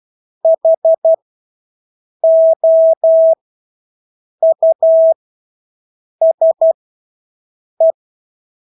Morse code, heard as a keyed tone, about 12 wpm, spaced as for 6 wpm, 650 Hz. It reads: HOUSE
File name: Engelske ord 12-6 wpm